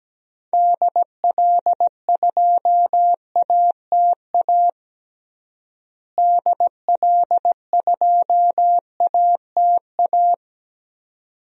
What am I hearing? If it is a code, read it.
Morse code, heard as a keyed tone, about 17 wpm, 700 Hz. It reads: DL2ATA DL2ATA
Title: DXCC-100 Call Signs - 6 Characters - Repeated 2x - 17wpm